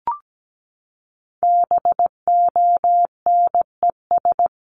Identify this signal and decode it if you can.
Morse code, heard as a keyed tone, about 17 wpm, 700 Hz. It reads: BONES
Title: A Sound Of Thunder 17wpm